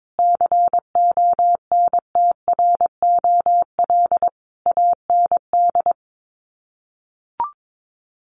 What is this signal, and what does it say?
Morse code, heard as a keyed tone, about 22 wpm, 700 Hz. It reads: CONTROL AND
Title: Sets of 5 Words Encore2 22wpm